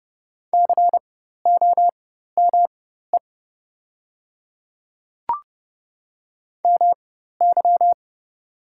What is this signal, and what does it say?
Morse code, heard as a keyed tone, about 30 wpm, spaced as for 14 wpm, 690 Hz. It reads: COME MY
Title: 4x Character Spacing - Top 100 Words - 30wpm